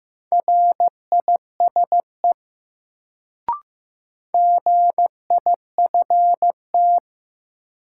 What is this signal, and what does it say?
Morse code, heard as a keyed tone, about 15 wpm, 700 Hz. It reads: RISE GIFT
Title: Sentences from Top 900 Words - New Word Drill - 15wpm